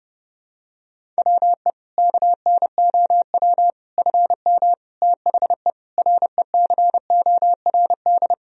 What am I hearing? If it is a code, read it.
Morse code, heard as a keyed tone, about 30 wpm, 700 Hz. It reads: WE KNOW FM THE RECORD
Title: Sentences from Top 500 Words and Abbreviations 30wpm